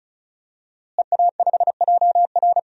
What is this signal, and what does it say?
Morse code, heard as a keyed tone, about 36 wpm, 700 Hz. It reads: EA5JR
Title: DXCC-100 Call Signs - 5 Characters - Repeated 2x - 35wpm